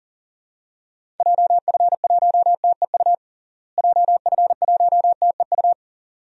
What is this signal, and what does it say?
Morse code, heard as a keyed tone, about 40 wpm, 700 Hz. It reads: JF1TEU JF1TEU
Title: DXCC-100 Call Signs - 6 Characters - Repeated 2x - 40wpm